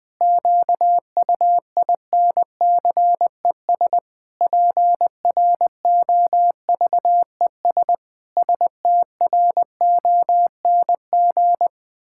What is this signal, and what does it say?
Morse code, heard as a keyed tone, about 20 wpm, 700 Hz. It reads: QUINCES PROVES STRONG
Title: Pangram Challenge v2 - 20wpm